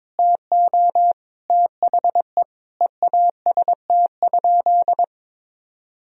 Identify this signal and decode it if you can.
Morse code, heard as a keyed tone, about 22 wpm, 700 Hz. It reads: TO THE EAST?
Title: Sentences from Top 1000 Words - Review Part 3 - 22wpm